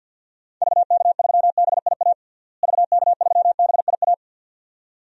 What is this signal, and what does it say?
Morse code, heard as a keyed tone, about 52 wpm, 700 Hz. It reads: VK3BIA VK3BIA
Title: DXCC-100 Call Signs - 6 Characters - Repeated 2x - 50wpm